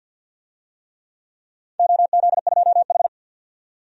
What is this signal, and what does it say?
Morse code, heard as a keyed tone, about 47 wpm, 700 Hz. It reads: OZ2H
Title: DXCC-100 Call Signs - 4 Characters - 50wpm